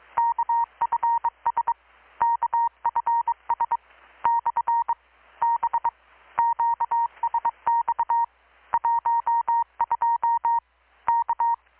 Morse code, heard as a keyed tone, about 22 wpm, 930 Hz. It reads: KFS KFS / B QSX 12 K
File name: blog / swl / 6-abril-2019-12695-USB-KPS